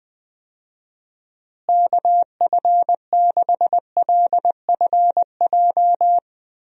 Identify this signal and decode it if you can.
Morse code, heard as a keyed tone, about 20 wpm, 700 Hz. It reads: KF6LFJ
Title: USA Call Signs - 6 Characters - Repeated 2x - 20wpm